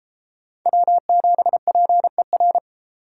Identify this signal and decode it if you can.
Morse code, heard as a keyed tone, about 30 wpm, 700 Hz. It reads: W7PER